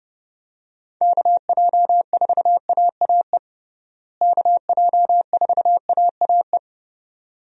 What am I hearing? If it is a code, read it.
Morse code, heard as a keyed tone, about 32 wpm, 700 Hz. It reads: KJ4AAE KJ4AAE
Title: USA Call Signs - 6 Characters - Repeated 2x - 30wpm